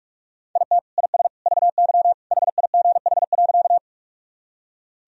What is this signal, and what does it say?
Morse code, heard as a keyed tone, about 46 wpm, 700 Hz. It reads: IT IS VY HIGH.
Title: Sentences from Top 200 Words and Abbreviations 45wpm